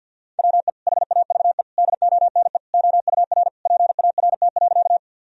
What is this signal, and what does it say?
Morse code, heard as a keyed tone, about 51 wpm, 700 Hz. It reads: WE HAVE DONE OUR PART.
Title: Sentences from Top 1000 Words - Review Part 1 - 50wpm